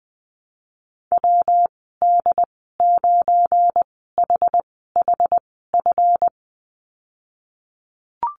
Morse code, heard as a keyed tone, about 20 wpm, spaced as for 15 wpm, 700 Hz. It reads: WD9HHF
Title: 173 Lesson 28g - Number 9 - Call Signs 20wpm